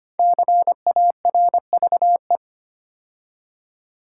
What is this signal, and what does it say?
Morse code, heard as a keyed tone, about 25 wpm, 700 Hz. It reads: CARVE
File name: 5 Letter Words - First 3 Letters Given - 25wpm